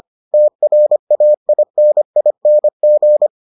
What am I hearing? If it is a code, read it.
Morse code, heard as a keyed tone, about 25 wpm, 600 Hz. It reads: TRAINING